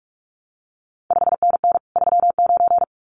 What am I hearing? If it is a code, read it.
Morse code, heard as a keyed tone, about 46 wpm, 700 Hz. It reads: 5NN 39
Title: CQ WW DX Contest Exchange Encore - 45wpm